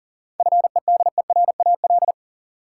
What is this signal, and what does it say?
Morse code, heard as a keyed tone, about 40 wpm, 700 Hz. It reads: FEDERAL
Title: Sentences from Top 1000 Words - Review Part 5 - New Word Drill - 40wpm